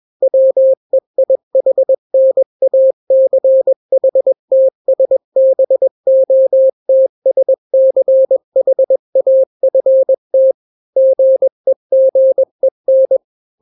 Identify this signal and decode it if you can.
Morse code, heard as a keyed tone, about 21 wpm, 530 Hz. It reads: WEIHNACHTSBOTSCHAFT GEGEN
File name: daily / 20wpm